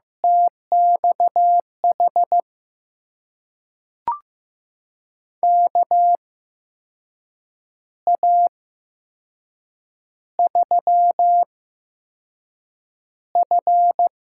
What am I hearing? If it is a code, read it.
Morse code, heard as a keyed tone, about 15 wpm, spaced as for 6 wpm, 700 Hz. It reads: TXH KA3F
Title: USA Call Signs - 6 Characters - 8x Character Spacing - 15wpm